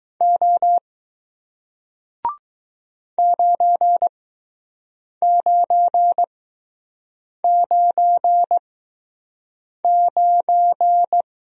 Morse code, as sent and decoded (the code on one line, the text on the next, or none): --- / ----. ----. ----. ----.
O 9999